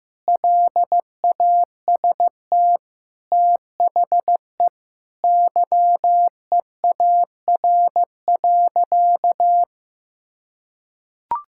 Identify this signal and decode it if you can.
Morse code, heard as a keyed tone, about 16 wpm, 700 Hz. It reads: LAST THE YEAR.
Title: Sentences from Top 1600 Words 15wpm